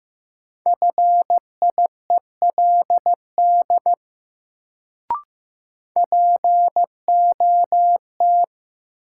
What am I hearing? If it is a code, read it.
Morse code, heard as a keyed tone, about 15 wpm, 700 Hz. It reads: FIELD POT
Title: Sentences from Top 1000 Words - New Word Drill - 15wpm